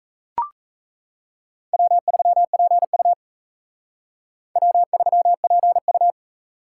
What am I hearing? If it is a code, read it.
Morse code, heard as a keyed tone, about 35 wpm, 700 Hz. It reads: W3PU W3PU
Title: Speed Racing - USA 1x2 Call Signs 28wpm